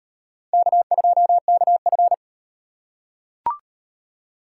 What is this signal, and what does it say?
Morse code, heard as a keyed tone, about 38 wpm, 700 Hz. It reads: K2KF